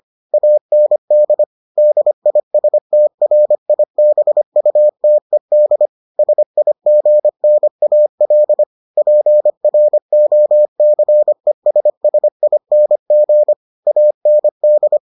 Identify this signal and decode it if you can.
Morse code, heard as a keyed tone, about 25 wpm, 600 Hz. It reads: AND DISTRIBUTED SIGNAL PROCESSING AND